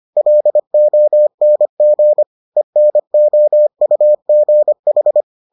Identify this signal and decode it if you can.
Morse code, heard as a keyed tone, about 25 wpm, 600 Hz. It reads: LONG ENOUGH